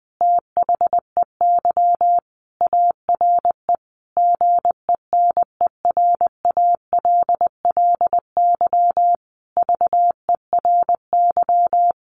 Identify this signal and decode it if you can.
Morse code, heard as a keyed tone, about 20 wpm, 700 Hz. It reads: THEY ARE GENERALLY VERY
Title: Sentences from Top 1500 Words 20wpm